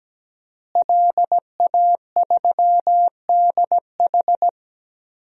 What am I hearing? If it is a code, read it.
Morse code, heard as a keyed tone, about 17 wpm, 700 Hz. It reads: LA3DH
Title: DXCC-100 Call Signs - 5 Characters - Repeated 2x - 17wpm